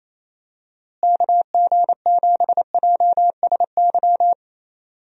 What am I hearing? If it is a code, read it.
Morse code, heard as a keyed tone, about 28 wpm, 700 Hz. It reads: KG7JSY
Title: USA Call Signs - 6 Characters - Repeated 2x - 28wpm